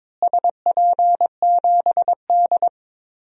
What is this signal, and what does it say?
Morse code, heard as a keyed tone, about 22 wpm, 700 Hz. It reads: SP7D
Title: DXCC-100 Call Signs - 4 Characters - Repeated 3x - 22wpm